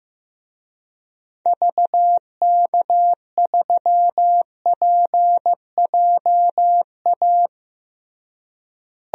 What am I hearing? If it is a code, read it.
Morse code, heard as a keyed tone, about 15 wpm, 700 Hz. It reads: VK3PJA
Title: DXCC-100 Call Signs - 6 Characters - Repeated 2x - 15wpm